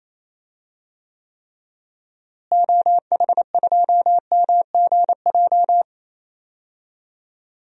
Morse code, heard as a keyed tone, about 28 wpm, 700 Hz. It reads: OH2MGJ
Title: DXCC-100 Call Signs - 6 Characters - Repeated 3x - 28wpm